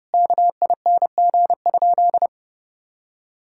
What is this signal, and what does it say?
Morse code, heard as a keyed tone, about 30 wpm, 700 Hz. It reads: KING?
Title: Sentences from Top 1000 Words - Review Part 5 - 30wpm